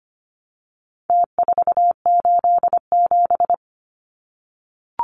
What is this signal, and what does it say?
Morse code, heard as a keyed tone, about 25 wpm, 700 Hz. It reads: T487
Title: Numbers - Contest Style - 4-Digit 25wpm